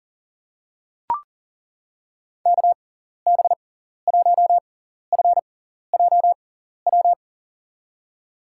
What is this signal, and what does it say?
Morse code, heard as a keyed tone, about 40 wpm, spaced as for 14 wpm, 700 Hz. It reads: KB1FJW
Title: USA Call Signs - 6 Characters - 6x Character Spacing - 40wpm